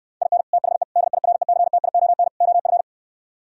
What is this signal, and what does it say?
Morse code, heard as a keyed tone, about 68 wpm, 700 Hz. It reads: IN THE DIRECTION OF